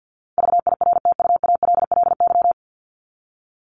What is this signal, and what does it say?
Morse code, heard as a keyed tone, about 50 wpm, 700 Hz. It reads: VIRTUALLY
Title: LY Suffix - Top 100 Words with Base Words 50wpm